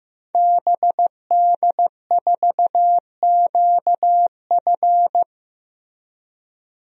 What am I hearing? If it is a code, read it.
Morse code, heard as a keyed tone, about 15 wpm, 700 Hz. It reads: BD4QF
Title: DXCC-100 Call Signs - 5 Characters - Repeated 3x - 15wpm